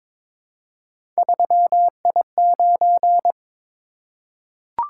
Morse code, heard as a keyed tone, about 21 wpm, 700 Hz. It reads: 3I9
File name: practice / 22wpm / Three Letter-Number Combinations - 22wpm